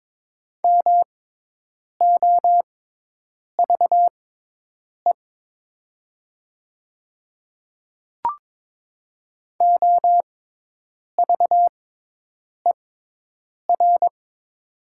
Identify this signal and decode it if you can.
Morse code, heard as a keyed tone, about 22 wpm, spaced as for 8 wpm, 700 Hz. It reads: MOVE OVER
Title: 6x Character Spacing - Top 300 Words - 22wpm